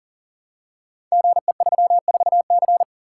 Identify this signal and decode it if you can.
Morse code, heard as a keyed tone, about 39 wpm, 700 Hz. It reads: GE34C
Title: Code Groups - 5 Letter:Numbers - 40wpm